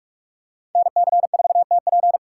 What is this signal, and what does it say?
Morse code, heard as a keyed tone, about 45 wpm, 700 Hz. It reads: NC4TP